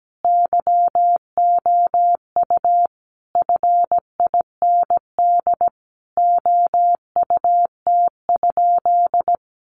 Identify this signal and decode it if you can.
Morse code, heard as a keyed tone, about 17 wpm, 700 Hz. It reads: YOU FIND OUT?
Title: Sentences from Top 200 Words 17wpm